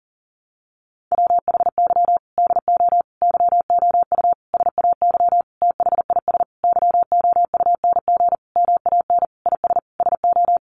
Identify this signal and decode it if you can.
Morse code, heard as a keyed tone, about 40 wpm, 700 Hz. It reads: WHY DO YOU SAY THIS YOUNG MAN IS SO
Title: Sentences from Top 1100 Words 40wpm